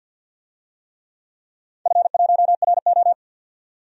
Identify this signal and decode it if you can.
Morse code, heard as a keyed tone, about 52 wpm, 700 Hz. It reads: F1RO
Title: DXCC-100 Call Signs - 4 Characters - 50wpm